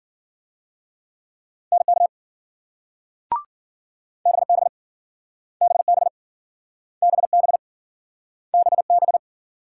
Morse code, heard as a keyed tone, about 53 wpm, 700 Hz. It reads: NK BB BB BB BB